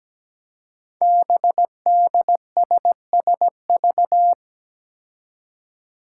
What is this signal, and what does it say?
Morse code, heard as a keyed tone, about 17 wpm, 700 Hz. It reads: BDSSV
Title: Five Challenging Letter-Number Combinations 17wpm